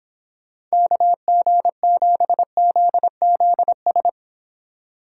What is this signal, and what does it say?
Morse code, heard as a keyed tone, about 26 wpm, 700 Hz. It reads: KG7ZZS